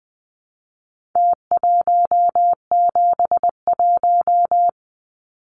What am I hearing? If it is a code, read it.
Morse code, heard as a keyed tone, about 20 wpm, 700 Hz. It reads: T171
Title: Numbers - Contest Style - 4-Digit 20wpm